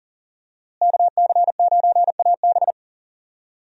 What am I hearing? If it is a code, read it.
Morse code, heard as a keyed tone, about 40 wpm, 700 Hz. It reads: KC9AB
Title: USA Call Signs - 5 Characters - Repeated 3x - 40wpm